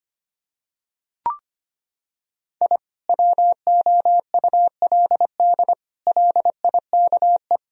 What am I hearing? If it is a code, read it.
Morse code, heard as a keyed tone, about 25 wpm, 700 Hz. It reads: I WOULD LIKE
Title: Sentences from Top 1000 Words - Review Part 1 - 25wpm